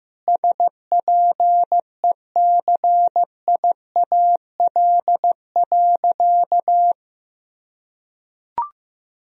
Morse code, heard as a keyed tone, about 15 wpm, 700 Hz. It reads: SPECIAL.
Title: Sentences from Top 1000 Words - Review Part 3 - 15wpm